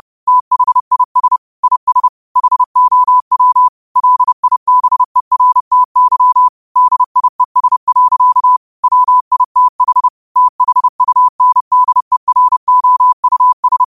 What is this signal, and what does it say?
Morse code, heard as a keyed tone, about 30 wpm, 1000 Hz. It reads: THIS IS HOW LIBERTY DIES. WITH THUNDEROUS
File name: qotdfiles / 30wpm